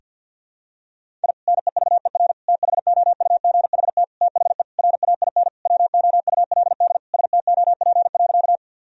Morse code, heard as a keyed tone, about 52 wpm, 700 Hz. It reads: I NEVER THOUGHT THE RAIN WOULD STOP.
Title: Sentences from Top 1000 Words - Review Part 1 - 50wpm